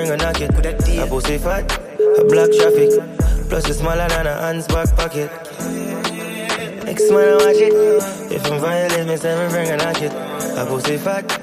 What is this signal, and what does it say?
Ringback tone (Europe, 425 Hz), with noise.